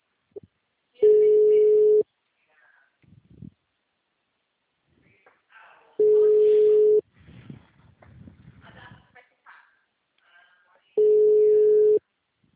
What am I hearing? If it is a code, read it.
Ringback tone (Europe, 425 Hz).